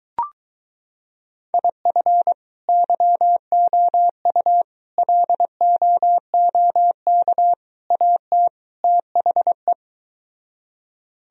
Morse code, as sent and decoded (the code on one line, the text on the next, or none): .. ..-. / -.-- --- ..- / .-.. --- --- -.- / .- - / - .... .
IF YOU LOOK AT THE